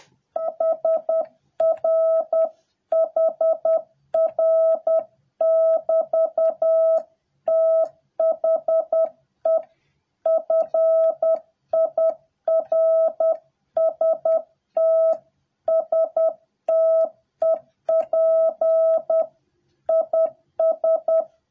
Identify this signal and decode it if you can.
Morse code, heard as a keyed tone, about 10 wpm, 650 Hz. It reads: SRH R<BT>THE FIRST STEP IS